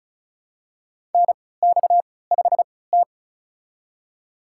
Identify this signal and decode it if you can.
Morse code, heard as a keyed tone, about 36 wpm, spaced as for 20 wpm, 700 Hz. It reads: NX5T
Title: USA Call Signs - 4 Characters - 3x Character Spacing - 35wpm